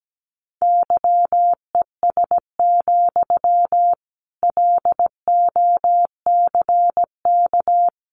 Morse code, heard as a keyed tone, about 17 wpm, 700 Hz. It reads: YES, LOCK
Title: Sentences from Top 1300 Words 17wpm